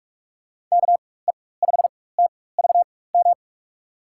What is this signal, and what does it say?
Morse code, heard as a keyed tone, about 46 wpm, spaced as for 21 wpm, 700 Hz. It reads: KE5TVM
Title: USA Call Signs - 4x Character Spacing - 45wpm